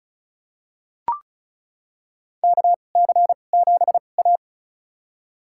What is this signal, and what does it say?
Morse code, heard as a keyed tone, about 36 wpm, spaced as for 25 wpm, 700 Hz. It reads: KC7A
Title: USA Call Signs - 4 Characters - 2x Character Spacing - 35wpm